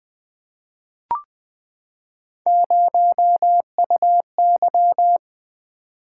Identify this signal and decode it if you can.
Morse code, heard as a keyed tone, about 20 wpm, 700 Hz. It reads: 0UY
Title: Three Letter-Number Combinations - 20wpm